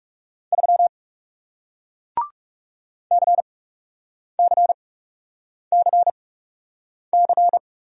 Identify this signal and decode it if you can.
Morse code, heard as a keyed tone, about 40 wpm, spaced as for 8 wpm, 700 Hz. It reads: 3 CCCC